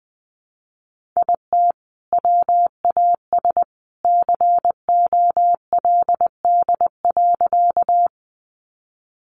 Morse code, heard as a keyed tone, about 20 wpm, 700 Hz. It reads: IT WAS COLD.
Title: Sentences from Top 400 Words 20wpm